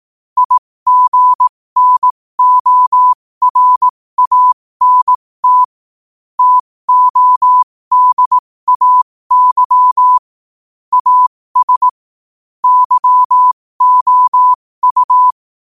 Morse code, heard as a keyed tone, about 18 wpm, spaced as for 15 wpm, 1000 Hz. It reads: IGNORANT TODAY AS YOU
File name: qotdfiles / 15wpm